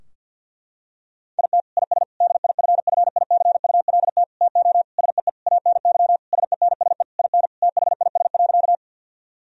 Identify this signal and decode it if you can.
Morse code, heard as a keyed tone, about 50 wpm, 700 Hz. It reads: IT IS DIFFICULT TO SEE ANY SENSE IN THIS.
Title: Sentences from Top 1000 Words - Review Part 4 - 50wpm